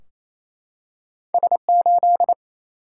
Morse code, heard as a keyed tone, about 27 wpm, 700 Hz. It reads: S8
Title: Single-Letter Single-Number 28wpm